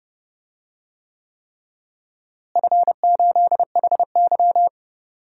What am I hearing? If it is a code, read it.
Morse code, heard as a keyed tone, about 30 wpm, 700 Hz. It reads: F8HY
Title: DXCC-100 Call Signs - 4 Characters - Repeated 3x - 30wpm